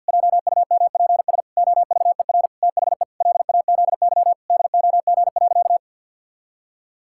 Morse code, heard as a keyed tone, about 50 wpm, 700 Hz. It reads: JUMPS OVER THE LAZY DOG.
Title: Pangram Challenge v2 - 50wpm